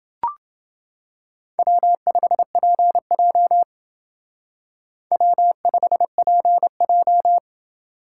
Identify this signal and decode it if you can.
Morse code, heard as a keyed tone, about 28 wpm, 700 Hz. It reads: W5PJ W5PJ